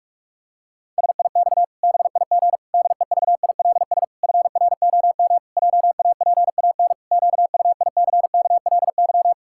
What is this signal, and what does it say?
Morse code, heard as a keyed tone, about 48 wpm, 700 Hz. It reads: SIX BIG DEVILS FROM JAPAN QUICKLY
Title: Pangram Challenge v2 - 45wpm